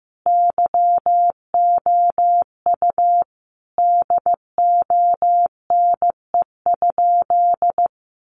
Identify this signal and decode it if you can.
Morse code, heard as a keyed tone, about 15 wpm, 700 Hz. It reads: YOU DONE?
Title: Sentences from Top 500 Words 15wpm